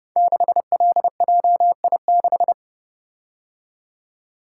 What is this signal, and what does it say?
Morse code, heard as a keyed tone, about 31 wpm, 700 Hz. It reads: 6LJI6
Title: Code Groups - 5 Letter:Numbers - 30wpm